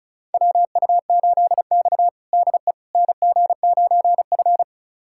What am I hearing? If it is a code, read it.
Morse code, heard as a keyed tone, about 36 wpm, 700 Hz. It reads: WU8X DE NG9F
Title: USA Callsign DE Callsign - 4 Characters - 35wpm